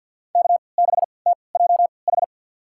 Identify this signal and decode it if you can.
Morse code, heard as a keyed tone, about 52 wpm, spaced as for 29 wpm, 700 Hz. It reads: K6TJH
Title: USA Call Signs - 5 Characters - 3x Character Spacing - 50wpm